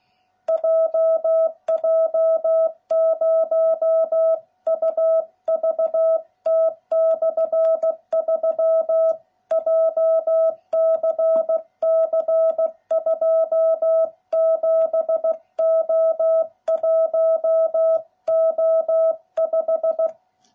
Morse code, heard as a keyed tone, about 16 wpm, 660 Hz. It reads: JJ0UVT/3 JCC 27O1O5